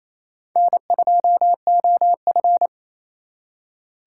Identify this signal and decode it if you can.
Morse code, heard as a keyed tone, about 34 wpm, 700 Hz. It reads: N2OF
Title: Speed Racing - USA 1x2 Call Signs 28wpm